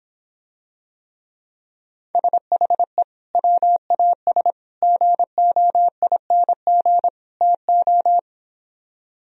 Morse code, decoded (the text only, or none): SHE WAS GOING TO